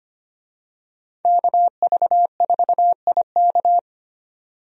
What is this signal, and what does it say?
Morse code, heard as a keyed tone, about 25 wpm, 690 Hz. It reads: KV4IK
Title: USA Call Signs - 5 Characters - 25wpm